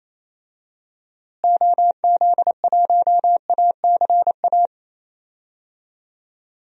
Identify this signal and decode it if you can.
Morse code, heard as a keyed tone, about 28 wpm, 700 Hz. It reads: OZ1ACA
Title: DXCC-100 Call Signs - 6 Characters - Repeated 3x - 28wpm